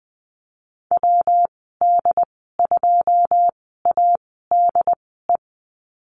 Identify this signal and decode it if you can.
Morse code, heard as a keyed tone, about 20 wpm, spaced as for 14 wpm, 700 Hz. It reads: WD2ADE
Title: 122 Lesson 20g - Number 3 - Call Signs 20wpm